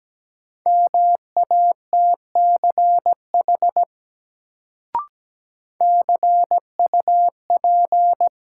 Morse code, heard as a keyed tone, about 17 wpm, 700 Hz. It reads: MATCH CUP
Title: Sentences from Top 1600 Words - New Word Drill - 17wpm